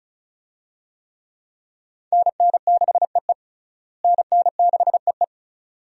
Morse code, heard as a keyed tone, about 36 wpm, 700 Hz. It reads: NN6EE NN6EE
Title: USA Call Signs - 5 Characters - Repeated 2x - 35wpm